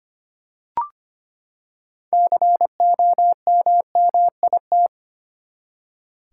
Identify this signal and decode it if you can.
Morse code, heard as a keyed tone, about 25 wpm, 700 Hz. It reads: COMMIT
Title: Six Letter Words - 25wpm